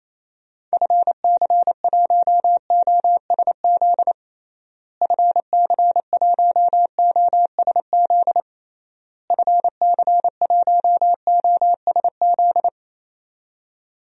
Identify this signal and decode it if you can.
Morse code, heard as a keyed tone, about 28 wpm, 700 Hz. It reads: FC1OSZ FC1OSZ FC1OSZ
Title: DXCC-100 Call Signs - 6 Characters - Repeated 3x - 28wpm